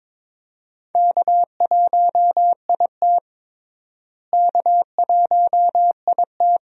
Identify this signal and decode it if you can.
Morse code, heard as a keyed tone, about 22 wpm, 700 Hz. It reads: K1IT K1IT
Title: USA Call Signs - 4 Characters - Repeated 2x - 22wpm